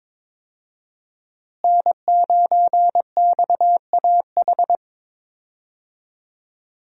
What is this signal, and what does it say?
Morse code, heard as a keyed tone, about 22 wpm, 700 Hz. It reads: N9XAH
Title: USA Call Signs - 5 Characters - Repeated 2x - 22wpm